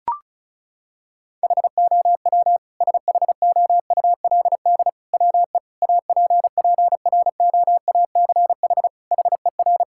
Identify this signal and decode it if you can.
Morse code, heard as a keyed tone, about 34 wpm, 700 Hz. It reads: HOW SHOULD WE APPROACH HER
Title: Sentences from Top 1000 Words - Review Part 6 - 35wpm